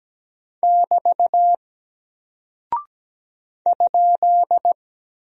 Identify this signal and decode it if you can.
Morse code, heard as a keyed tone, about 17 wpm, 700 Hz. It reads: <BT> ?
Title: Single Character (Letter:Number:Punctuation:Prosign) Alt 17wpm